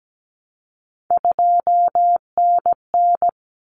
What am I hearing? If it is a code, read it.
Morse code, heard as a keyed tone, about 17 wpm, 700 Hz. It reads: 2NN
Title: Common Cut Numbers - 3-Digit 17wpm